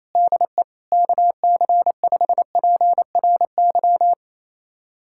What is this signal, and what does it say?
Morse code, heard as a keyed tone, about 28 wpm, 700 Hz. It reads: DE KC5PRY
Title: USA Callsign DE Callsign - 6 Characters - 28wpm